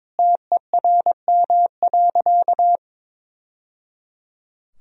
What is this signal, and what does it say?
Morse code, heard as a keyed tone, about 22 wpm, 700 Hz. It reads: TERM.
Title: Sentences from Top 1000 Words - Review Part 4 - 22wpm